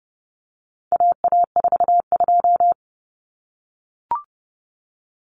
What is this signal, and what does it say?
Morse code, heard as a keyed tone, about 30 wpm, 700 Hz. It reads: AA42
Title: Common Cut Numbers - 4-Digit 30wpm